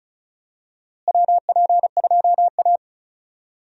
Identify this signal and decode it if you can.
Morse code, heard as a keyed tone, about 34 wpm, 700 Hz. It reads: WP2A
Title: USA Call Signs - 4 Characters - Repeated 3x - 35wpm